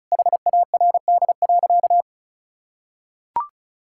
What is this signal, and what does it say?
Morse code, heard as a keyed tone, about 36 wpm, 700 Hz. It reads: HARD.
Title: Pangram Challenge v2 - 35wpm